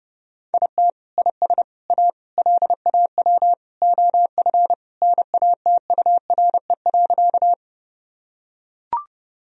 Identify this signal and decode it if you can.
Morse code, heard as a keyed tone, about 30 wpm, 700 Hz. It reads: IT IS A LAW OF NATURE.
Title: Sentences from Top 1000 Words - Review Part 3 - 30wpm